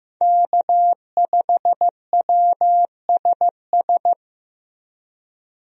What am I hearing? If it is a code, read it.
Morse code, heard as a keyed tone, about 15 wpm, 700 Hz. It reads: K5WSS
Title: USA Call Signs - 5 Characters - Repeated 2x - 15wpm